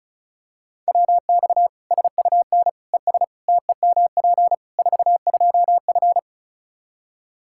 Morse code, heard as a keyed tone, about 34 wpm, 700 Hz. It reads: WX SUN ES TEMP 42F
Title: K1USN SST QSO Practice 35wpm